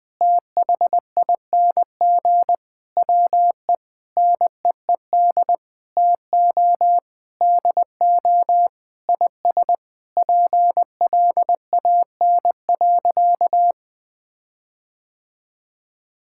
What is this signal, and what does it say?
Morse code, heard as a keyed tone, about 21 wpm, 700 Hz. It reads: THING WE NEED TO DO IS PLAN.
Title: Sentences with 8-Word Phrases - Spoken Beginning - 20wpm